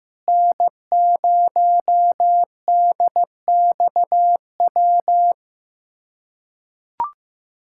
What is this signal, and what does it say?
Morse code, heard as a keyed tone, about 15 wpm, 700 Hz. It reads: N0DXW
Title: USA Callsign DE Callsign - 5 Characters - 15wpm